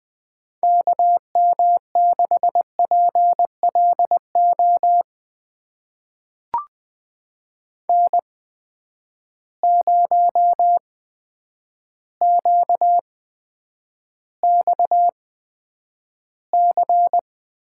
Morse code, heard as a keyed tone, about 20 wpm, spaced as for 9 wpm, 700 Hz. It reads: KM6PLO N0QXC
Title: USA Call Signs - 8x Character Spacing - 20wpm